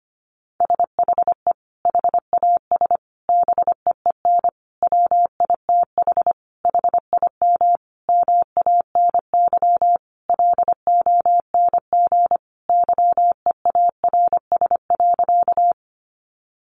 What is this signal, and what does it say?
Morse code, heard as a keyed tone, about 25 wpm, 700 Hz. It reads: SHE HAS BEEN WITH HIM MANY LONG YEARS.
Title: Sentences from Top 200 Words 25wpm